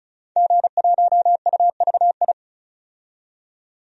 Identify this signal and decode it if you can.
Morse code, heard as a keyed tone, about 36 wpm, 700 Hz. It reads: G1UVI
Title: DXCC-100 Call Signs - 5 Characters - 35wpm